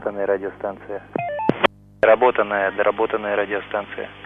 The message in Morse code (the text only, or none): none